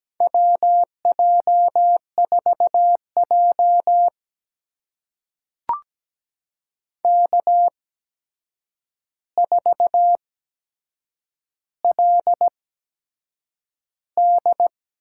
Morse code, heard as a keyed tone, about 17 wpm, spaced as for 7 wpm, 700 Hz. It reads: WJ4J K4LD
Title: USA Call Signs - 4 Characters - 8x Character Spacing - 17wpm